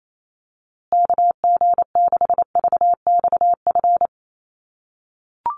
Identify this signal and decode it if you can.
Morse code, heard as a keyed tone, about 28 wpm, 700 Hz. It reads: KG6VXF
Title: USA Call Signs Repeated 2x 28wpm